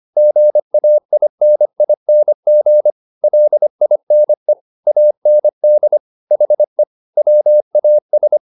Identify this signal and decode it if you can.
Morse code, heard as a keyed tone, about 25 wpm, 600 Hz. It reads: GAINING LINE AND HE WAS